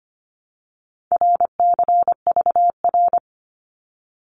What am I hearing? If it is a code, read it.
Morse code, heard as a keyed tone, about 25 wpm, 700 Hz. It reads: RCVR
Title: Abbreviations 25wpm